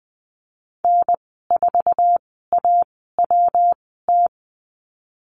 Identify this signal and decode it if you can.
Morse code, heard as a keyed tone, about 20 wpm, spaced as for 14 wpm, 700 Hz. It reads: N4AWT
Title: 122 Lesson 20g - Number 3 - Call Signs 20wpm